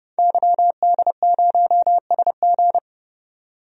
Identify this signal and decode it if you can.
Morse code, heard as a keyed tone, about 31 wpm, 700 Hz. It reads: YD0SG
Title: DXCC-100 Call Signs - 5 Characters - 30wpm